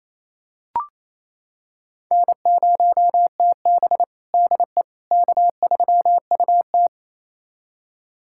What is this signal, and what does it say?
Morse code, heard as a keyed tone, about 28 wpm, 700 Hz. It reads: N0TB DE K3UT
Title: USA Callsign DE Callsign - 4 Characters - 28wpm